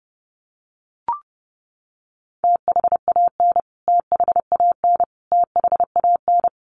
Morse code, heard as a keyed tone, about 30 wpm, 700 Hz. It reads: THAN THAN THAN
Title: Top 300 Words Repeated 30wpm